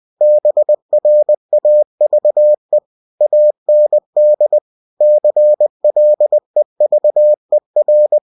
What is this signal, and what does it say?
Morse code, heard as a keyed tone, about 20 wpm, 600 Hz. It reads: BRAVE AND CLEVER
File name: cw-books / winnie-the-pooh / winnie-20wpm-page-0099